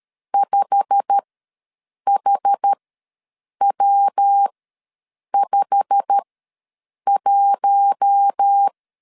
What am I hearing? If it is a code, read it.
Morse code, heard as a keyed tone, about 13 wpm, spaced as for 7 wpm, 800 Hz. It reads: 5HW51
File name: LEZIONI / 8 YJ45 / 8 YJ45 ws 12wpm 3146